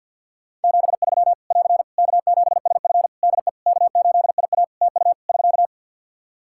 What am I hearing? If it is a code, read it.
Morse code, heard as a keyed tone, about 51 wpm, 700 Hz. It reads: 73 <AR> K7SF DE K8IA TU <SK>
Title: Practice QSO - Exchange 4 - Ending - 50wpm